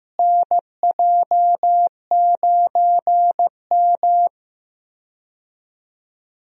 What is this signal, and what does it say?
Morse code, heard as a keyed tone, about 15 wpm, 700 Hz. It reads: NJ9M
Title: USA Call Signs - 4 Characters - Repeated 3x - 15wpm